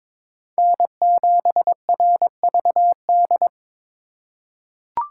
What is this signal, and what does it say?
Morse code, heard as a keyed tone, about 22 wpm, 700 Hz. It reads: N7RVD
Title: K1USN SST QSO Practice - 22wpm